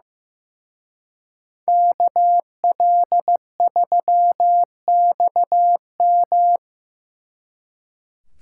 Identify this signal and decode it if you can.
Morse code, heard as a keyed tone, about 18 wpm, 700 Hz. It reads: KL3XM
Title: Speed Racing - USA Call Signs 15wpm